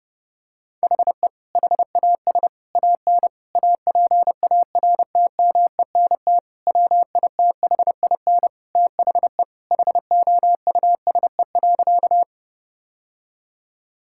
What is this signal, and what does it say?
Morse code, heard as a keyed tone, about 32 wpm, 700 Hz. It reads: HE HAS AN APARTMENT WITHIN THE HOUSE.
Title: Sentences from Top 1000 Words - Review Part 6 - 30wpm